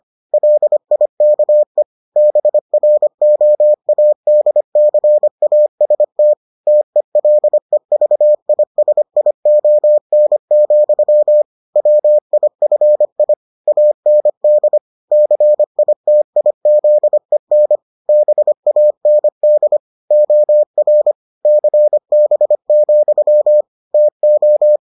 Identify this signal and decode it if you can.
Morse code, heard as a keyed tone, about 25 wpm, 600 Hz. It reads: LIKE BROADCAST TELEVISION, WIFI AND CITIZEN BAND OR CB, TO